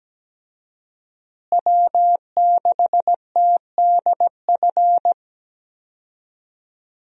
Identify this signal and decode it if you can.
Morse code, heard as a keyed tone, about 17 wpm, 700 Hz. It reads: W6TDF
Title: USA Call Signs - 5 Characters - Repeated 3x - 17wpm